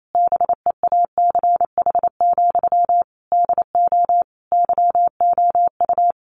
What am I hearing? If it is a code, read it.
Morse code, heard as a keyed tone, about 28 wpm, 700 Hz. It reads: BEACH, DO YOU
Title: Sentences from Top 1500 Words 28wpm